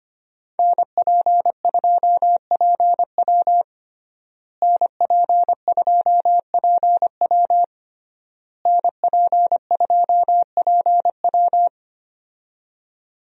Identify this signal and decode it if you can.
Morse code, heard as a keyed tone, about 25 wpm, 700 Hz. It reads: NP2PW NP2PW NP2PW
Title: USA Call Signs - 5 Characters - Repeated 3x - 25wpm